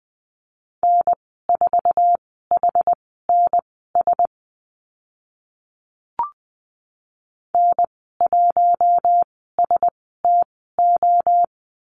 Morse code, heard as a keyed tone, about 19 wpm, spaced as for 14 wpm, 700 Hz. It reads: N4HNS N1STO
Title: 062 Lesson 11g - Letter h - Call Signs 20wpm